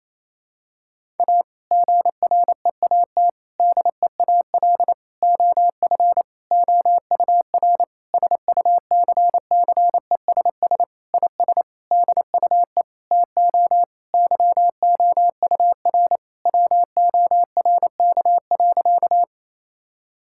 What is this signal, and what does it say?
Morse code, heard as a keyed tone, about 28 wpm, 690 Hz. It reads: A GREAT DEAL OF OUR SUCCESS IS DUE TO YOUR WORK.
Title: Sentences from Top 1000 Words - Review Part 6 - 28wpm